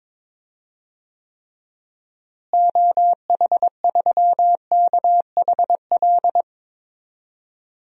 Morse code, heard as a keyed tone, about 22 wpm, 700 Hz. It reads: OH3KHL
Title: DXCC-100 Call Signs - 6 Characters - Repeated 3x - 22wpm